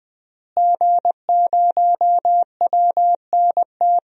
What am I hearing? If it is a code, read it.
Morse code, heard as a keyed tone, about 20 wpm, 700 Hz. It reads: G0WNT
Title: DXCC-100 Call Signs - 5 Characters - Repeated 2x - 20wpm